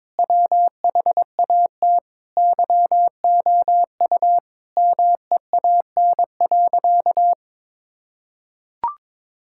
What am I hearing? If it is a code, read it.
Morse code, heard as a keyed tone, about 22 wpm, 700 Hz. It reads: WHAT YOU MEAN.
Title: Sentences from Top 1000 Words - Review Part 6 - 22wpm